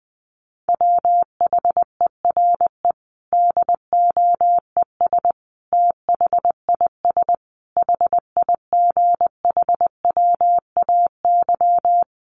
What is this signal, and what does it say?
Morse code, heard as a keyed tone, about 20 wpm, 700 Hz. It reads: WHERE DOES THIS HIGHWAY
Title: Sentences from Top 1400 Words 20wpm